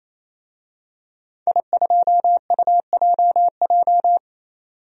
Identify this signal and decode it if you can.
Morse code, heard as a keyed tone, about 28 wpm, 700 Hz. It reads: I2UJJ
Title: DXCC-100 Call Signs - 5 Characters - 28wpm